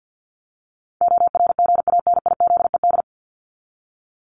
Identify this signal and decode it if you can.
Morse code, heard as a keyed tone, about 50 wpm, 700 Hz. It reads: ORGANIZED
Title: Top 1001-1250 Words 50wpm